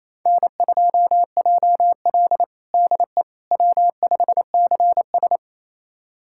Morse code, heard as a keyed tone, about 28 wpm, 700 Hz. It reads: N2JL DE W5CS
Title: USA Callsign DE Callsign - 4 Characters - 28wpm